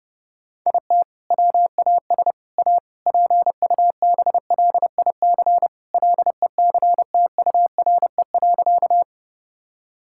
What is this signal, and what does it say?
Morse code, heard as a keyed tone, about 30 wpm, 700 Hz. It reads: IT WAS A PUBLIC LECTURE.
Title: Sentences from Top 1600 Words 30wpm